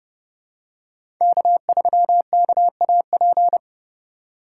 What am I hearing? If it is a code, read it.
Morse code, heard as a keyed tone, about 30 wpm, 700 Hz. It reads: K3KAP
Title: USA Call Signs - 5 Characters - 3x Character Spacing - 30wpm